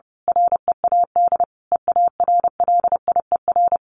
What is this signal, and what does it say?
Morse code, heard as a keyed tone, about 30 wpm, 700 Hz. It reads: READ EARLIER
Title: A Sound Of Thunder 30wpm